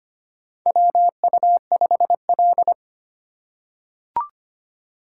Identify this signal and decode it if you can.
Morse code, heard as a keyed tone, about 25 wpm, 700 Hz. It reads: WU5L
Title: USA Call Signs - 4 Characters - 2x Character Spacing - 25wpm